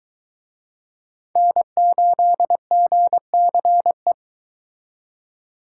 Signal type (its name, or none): Morse code